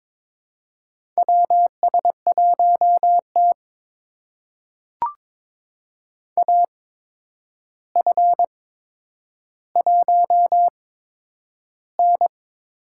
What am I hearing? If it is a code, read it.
Morse code, heard as a keyed tone, about 22 wpm, spaced as for 9 wpm, 700 Hz. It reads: WS1T AF1N
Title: USA Call Signs - 4 Characters - 8x Character Spacing - 22wpm